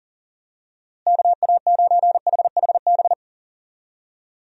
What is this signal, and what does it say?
Morse code, heard as a keyed tone, about 40 wpm, 700 Hz. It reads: KA9HHB
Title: USA Call Signs - 6 Characters - Repeated 3x - 40wpm